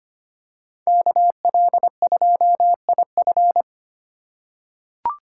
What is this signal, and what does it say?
Morse code, heard as a keyed tone, about 25 wpm, 700 Hz. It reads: KL2IF
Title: USA Call Signs - 5 Characters - 2x Character Spacing - 25wpm